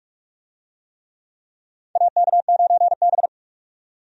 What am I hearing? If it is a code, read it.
Morse code, heard as a keyed tone, about 46 wpm, 700 Hz. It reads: AK9B
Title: USA Call Signs - 4 Characters - 45wpm